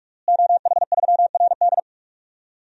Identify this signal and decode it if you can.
Morse code, heard as a keyed tone, about 47 wpm, 700 Hz. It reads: OH3RD
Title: DXCC-100 Call Signs - 5 Characters - Repeated 3x - 45wpm